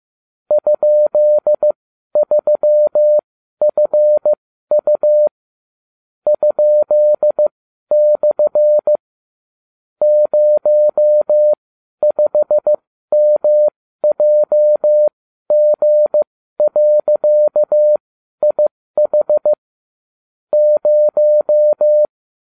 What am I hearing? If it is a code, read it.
Morse code, heard as a keyed tone, about 15 wpm, spaced as for 12 wpm, 600 Hz. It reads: ?3FU ?/ 05MJG.IH 0